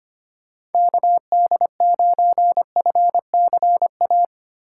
Morse code, heard as a keyed tone, about 32 wpm, 700 Hz. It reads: KD9FCA